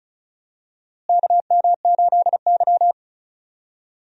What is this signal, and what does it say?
Morse code, heard as a keyed tone, about 34 wpm, 700 Hz. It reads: KM8Y
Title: USA Call Signs - 4 Characters - Repeated 2x - 35wpm